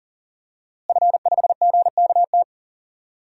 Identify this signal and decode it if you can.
Morse code, heard as a keyed tone, about 37 wpm, 700 Hz. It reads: F5GKT
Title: DXCC-100 Call Signs - 5 Characters - Repeated 3x - 40wpm